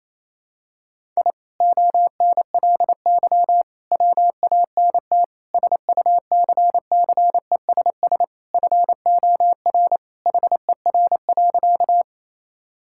Morse code, heard as a keyed tone, about 28 wpm, 700 Hz. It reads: I ONLY WANT SUCCESS FOR HER.
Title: Sentences from Top 1000 Words - Review Part 6 - 28wpm